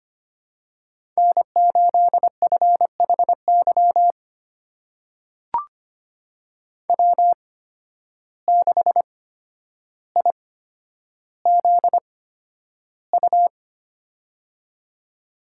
Morse code, heard as a keyed tone, about 25 wpm, spaced as for 11 wpm, 700 Hz. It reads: N8FHY W6IZU
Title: USA Call Signs - 5 Characters - 8x Character Spacing - 25wpm